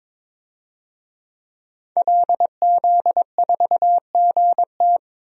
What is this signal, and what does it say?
Morse code, heard as a keyed tone, about 22 wpm, 700 Hz. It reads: LZ4GT
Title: DXCC-100 Call Signs - 5 Characters - 22wpm